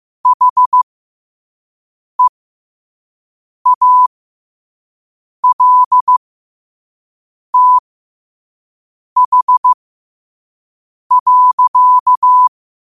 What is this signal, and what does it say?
Morse code, heard as a keyed tone, about 15 wpm, spaced as for 5 wpm, 1000 Hz. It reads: HEALTH.
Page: New quotes every day in morse code at 5 Words per minute.